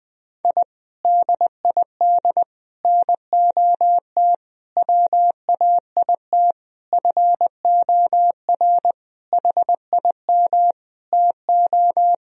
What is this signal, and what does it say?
Morse code, heard as a keyed tone, about 21 wpm, 700 Hz. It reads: I DID NOT WAIT FOR HIM TO
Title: Sentences from Top 1000 Words - Review Part 2 - 20wpm